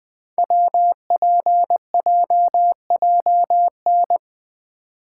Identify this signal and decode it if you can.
Morse code, heard as a keyed tone, about 20 wpm, 700 Hz. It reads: WPJJN
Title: Code Groups - 5 Letters - 20wpm